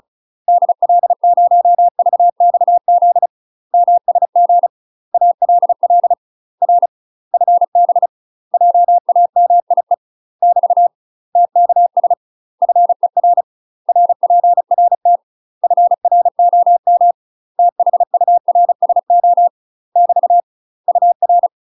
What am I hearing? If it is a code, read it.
Morse code, heard as a keyed tone, about 36 wpm, 700 Hz. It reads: DL0VXZ MSG ALL R FB JAMIE <BT> TKS FER RPRT FROM THURSO <BT> UR
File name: UK MASTER QSOs TRAINING / 35WPM / UK MASTER QSOs 0033